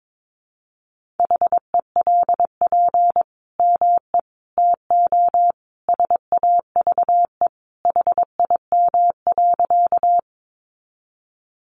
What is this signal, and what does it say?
Morse code, heard as a keyed tone, about 22 wpm, 700 Hz. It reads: HELP ME TO SAVE HIM.
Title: Sentences from Top 1100 Words 22wpm